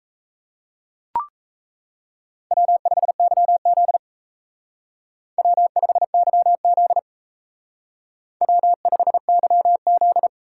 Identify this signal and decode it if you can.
Morse code, heard as a keyed tone, about 32 wpm, 700 Hz. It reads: W5YZ W5YZ W5YZ
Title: Speed Racing - USA 1x2 Call Signs 28wpm